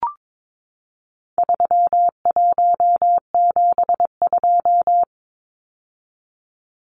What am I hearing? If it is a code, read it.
Morse code, heard as a keyed tone, about 22 wpm, 700 Hz. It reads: 3172
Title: Numbers - 4-Digit 22wpm